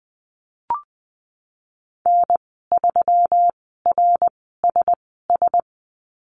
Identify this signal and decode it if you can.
Morse code, heard as a keyed tone, about 20 wpm, spaced as for 15 wpm, 700 Hz. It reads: N3RSS
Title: 154 Lesson 25g - Letter p - Call Signs 20wpm